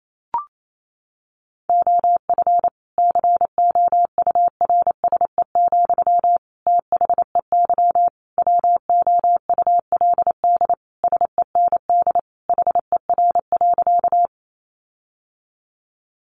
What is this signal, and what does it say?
Morse code, heard as a keyed tone, about 28 wpm, 700 Hz. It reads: OF COURSE, THEY WOULD SEND HER.
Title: Sentences from Top 1500 Words 28wpm